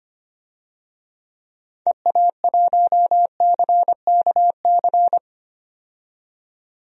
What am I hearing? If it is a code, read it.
Morse code, heard as a keyed tone, about 25 wpm, 700 Hz. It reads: EA1CKC
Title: DXCC-100 Call Signs - 6 Characters - 25wpm